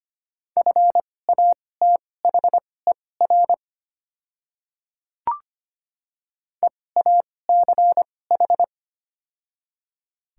Morse code, heard as a keyed tone, about 25 wpm, spaced as for 18 wpm, 700 Hz. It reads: FATHER EACH
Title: 2x Character Spacing - Top 200 Words - 25wpm